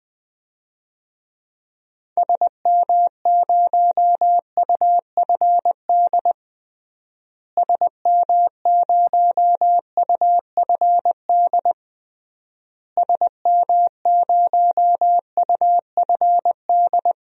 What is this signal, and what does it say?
Morse code, heard as a keyed tone, about 21 wpm, 700 Hz. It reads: SM0UFD SM0UFD SM0UFD
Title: DXCC-100 Call Signs - 6 Characters - Repeated 3x - 20wpm